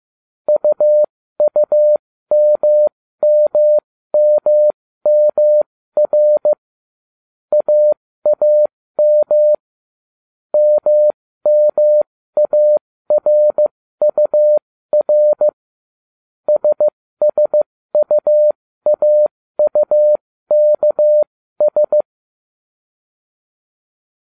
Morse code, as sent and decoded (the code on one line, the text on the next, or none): ..- ..- -- -- -- -- .-. / .- .- -- / -- -- .- .-. ..- .-. / ... ... ..- .- ..- -.- ...
UUMMMMR AAM MMARUR SSUAUKS